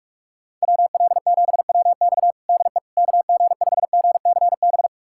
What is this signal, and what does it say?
Morse code, heard as a keyed tone, about 46 wpm, 700 Hz. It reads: WL7WX DE KG5GCB
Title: USA Callsign DE Callsign 45wpm